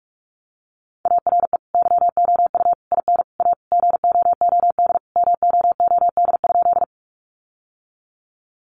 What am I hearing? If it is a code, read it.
Morse code, heard as a keyed tone, about 44 wpm, 700 Hz. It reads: ARE YOU IN A GOOD MOOD?
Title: Sentences from Top 1100 Words 45wpm